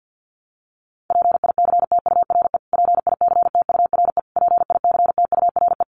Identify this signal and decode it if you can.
Morse code, heard as a keyed tone, about 49 wpm, 700 Hz. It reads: PICTURE PICTURE PICTURE
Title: Top 500 Words Repeated 50wpm